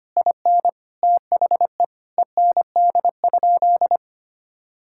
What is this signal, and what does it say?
Morse code, heard as a keyed tone, about 25 wpm, 700 Hz. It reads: IN THE END?
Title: Sentences from Top 1000 Words - Review Part 5 - 25wpm